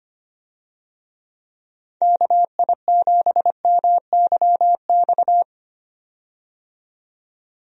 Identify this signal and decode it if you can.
Morse code, heard as a keyed tone, about 25 wpm, 700 Hz. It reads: KI7MYX
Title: USA Call Signs - 6 Characters - 25wpm